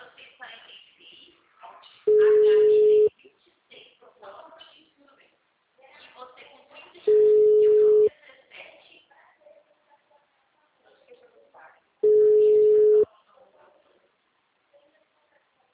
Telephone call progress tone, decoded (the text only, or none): ringback tone